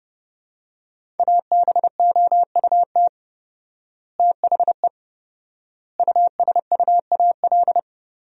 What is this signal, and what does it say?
Morse code, heard as a keyed tone, about 30 wpm, 700 Hz. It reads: ABOUT THE USUAL
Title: 4x Word-Spacing - Sentences from Top 400 Words - 30wpm